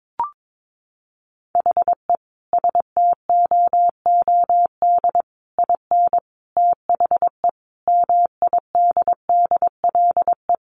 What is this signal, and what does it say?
Morse code, heard as a keyed tone, about 22 wpm, 700 Hz. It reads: HE STOOD IN THE MIDDLE
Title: Sentences from Top 1400 Words 22wpm